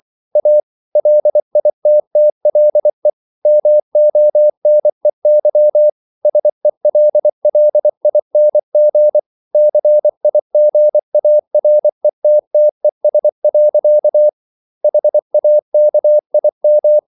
Morse code, heard as a keyed tone, about 24 wpm, 600 Hz. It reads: A LITTLE MONEY SELLING CIGARETTES. HAKIM